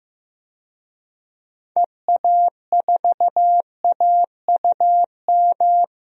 Morse code, heard as a keyed tone, about 16 wpm, 700 Hz. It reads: EA4AUM
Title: DXCC-100 Call Signs - 6 Characters - Repeated 3x - 15wpm